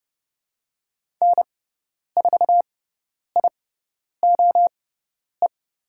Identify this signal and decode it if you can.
Morse code, heard as a keyed tone, about 30 wpm, spaced as for 10 wpm, 700 Hz. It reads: N4IOE
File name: Lesson 16g - Letter c - Call Signs 30wpm